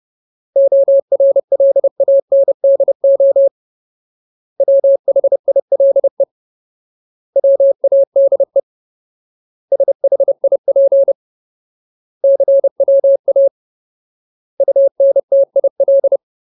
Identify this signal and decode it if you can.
Morse code, heard as a keyed tone, about 30 wpm, 550 Hz. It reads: ORLANDO WHILE WADE SHIP CWA UNTIL
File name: Ultimate Word List / 30WPM Ultimate